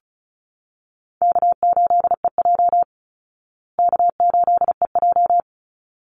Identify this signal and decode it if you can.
Morse code, heard as a keyed tone, about 34 wpm, 700 Hz. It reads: K8EJ K8EJ
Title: Easy Call Signs Repeated 2x - 35wpm